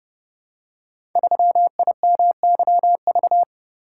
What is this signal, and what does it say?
Morse code, heard as a keyed tone, about 30 wpm, 700 Hz. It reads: 3IMYV
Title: Code Groups - 5 Letter:Numbers - 30wpm